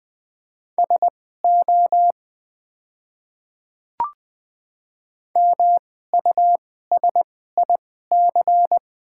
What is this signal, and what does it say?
Morse code, heard as a keyed tone, about 20 wpm, spaced as for 14 wpm, 700 Hz. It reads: SO MUSIC
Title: 2x Character Spacing - Top 300 Words - 20wpm